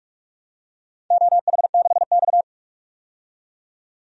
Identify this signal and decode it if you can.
Morse code, heard as a keyed tone, about 46 wpm, 700 Hz. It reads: OH6X
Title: DXCC-100 Call Signs - 4 Characters - 45wpm